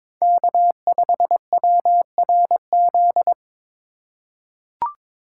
Morse code, heard as a keyed tone, about 22 wpm, 700 Hz. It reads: K5WRZ
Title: USA Call Signs - 5 Characters - 4x Character Spacing - 22wpm